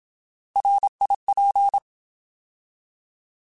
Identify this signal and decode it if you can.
Morse code, heard as a keyed tone, about 27 wpm, 800 Hz. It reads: RIP
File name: Alphabetical / Alphabetical mp3s / Rs @25